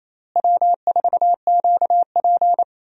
Morse code, heard as a keyed tone, about 28 wpm, 700 Hz. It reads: W4QP
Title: USA Call Signs - 4 Characters - Repeated 3x - 28wpm